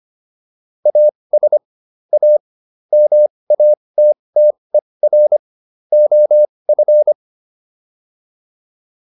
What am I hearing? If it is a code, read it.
Morse code, heard as a keyed tone, about 25 wpm, spaced as for 20 wpm, 600 Hz. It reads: AS A MATTER OF
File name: WD304 20